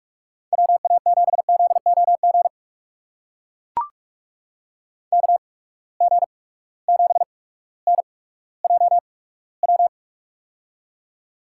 Morse code, heard as a keyed tone, about 41 wpm, spaced as for 19 wpm, 700 Hz. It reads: WA7ZOG KG7NJW
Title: USA Call Signs - 6 Characters - 8x Character Spacing - 45wpm